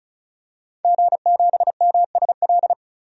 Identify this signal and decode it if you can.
Morse code, heard as a keyed tone, about 34 wpm, 700 Hz. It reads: G7MSL
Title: DXCC-100 Call Signs - 5 Characters - 35wpm